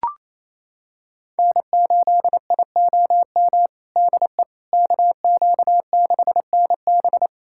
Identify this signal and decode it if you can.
Morse code, heard as a keyed tone, about 27 wpm, 700 Hz. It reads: N8IOM DE KQ6NB
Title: USA Callsign DE Callsign - 5 Characters - 28wpm